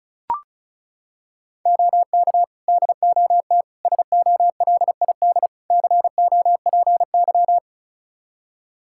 Morse code, heard as a keyed tone, about 36 wpm, 700 Hz. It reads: OK DOT SOLID COPY
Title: Practice QSO - Exchange 3 - More Info - 35wpm